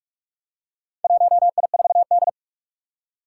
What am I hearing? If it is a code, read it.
Morse code, heard as a keyed tone, about 44 wpm, 700 Hz. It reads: 1I4D
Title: Four Challenging Letter-Number Combinations 45wpm